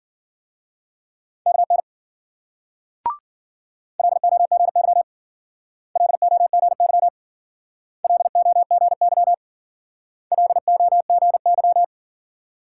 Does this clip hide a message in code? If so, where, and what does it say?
Yes: Morse code from 1.5 to 11.8 s, heard as a keyed tone, about 52 wpm, 700 Hz. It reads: KN LOGY LOGY LOGY LOGY